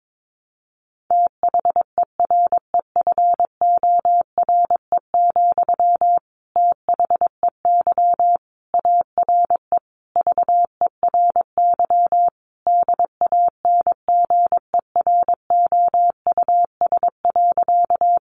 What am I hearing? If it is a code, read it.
Morse code, heard as a keyed tone, about 22 wpm, 700 Hz. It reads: THEREFORE, THEY ARE VERY DANGEROUS.
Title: Sentences from Top 1200 Words 22wpm